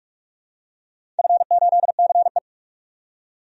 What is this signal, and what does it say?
Morse code, heard as a keyed tone, about 44 wpm, 700 Hz. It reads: F8CE
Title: DXCC-100 Call Signs - 4 Characters - 45wpm